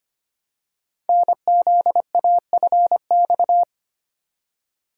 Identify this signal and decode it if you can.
Morse code, heard as a keyed tone, about 25 wpm, 700 Hz. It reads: NZAFX
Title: Code Groups - 5 Letters - 25wpm